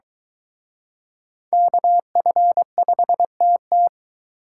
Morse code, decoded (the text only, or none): KF5TT